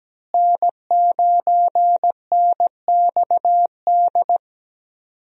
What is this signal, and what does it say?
Morse code, heard as a keyed tone, about 17 wpm, 700 Hz. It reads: N9NXD
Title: USA Call Signs - 5 Characters - Repeated 2x - 17wpm